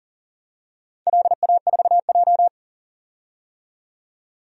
Morse code, heard as a keyed tone, about 39 wpm, 700 Hz. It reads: LA4J
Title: DXCC-100 Call Signs - 4 Characters - 40wpm